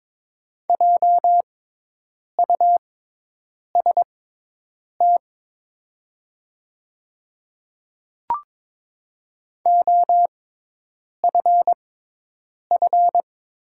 Morse code, heard as a keyed tone, about 22 wpm, spaced as for 8 wpm, 700 Hz. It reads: JUST OFF
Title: 6x Character Spacing - Top 500 Words - 22wpm